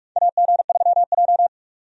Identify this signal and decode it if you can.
Morse code, heard as a keyed tone, about 46 wpm, 700 Hz. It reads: AG3J
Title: USA Call Signs - 4 Characters - 45wpm